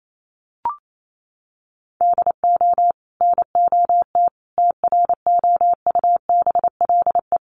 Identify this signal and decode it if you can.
Morse code, heard as a keyed tone, about 28 wpm, 700 Hz. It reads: DO NOT TROUBLE
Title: Sentences from Top 1200 Words 28wpm